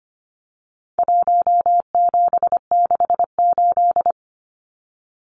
Morse code, heard as a keyed tone, about 25 wpm, 700 Hz. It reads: 1768
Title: Numbers - Contest Style - 4-Digit 25wpm